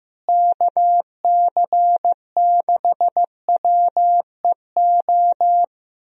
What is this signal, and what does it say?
Morse code, heard as a keyed tone, about 15 wpm, 700 Hz. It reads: KC6WEO
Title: USA Call Signs - 6 Characters - Repeated 2x - 15wpm